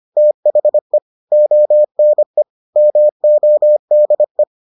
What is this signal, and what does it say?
Morse code, heard as a keyed tone, about 25 wpm, 600 Hz. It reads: THE ONE MODE